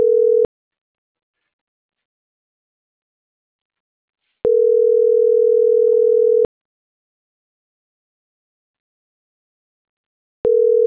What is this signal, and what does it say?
Ringback tone (North America), 440 + 480 Hz.
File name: CENTRAL NAVIERAS / out onelinecapacitacion - .30149